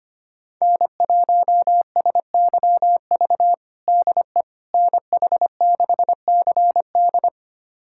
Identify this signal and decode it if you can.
Morse code, heard as a keyed tone, about 25 wpm, 700 Hz. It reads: N1SYV DE NH6CD
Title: USA Callsign DE Callsign - 5 Characters - 25wpm